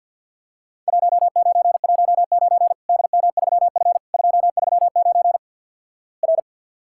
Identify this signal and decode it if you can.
Morse code, heard as a keyed tone, about 50 wpm, 700 Hz. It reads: 1919 DM3F 239 R
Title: Worked All Europe DX Contest - QTC Exchange - 50wpm